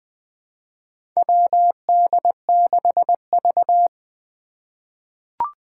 Morse code, heard as a keyed tone, about 21 wpm, 700 Hz. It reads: WD6V
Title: Four Challenging Letter-Number Combinations 20wpm